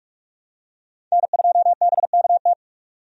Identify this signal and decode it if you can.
Morse code, heard as a keyed tone, about 46 wpm, 700 Hz. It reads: N2BKT
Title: USA Call Signs - 5 Characters - 45wpm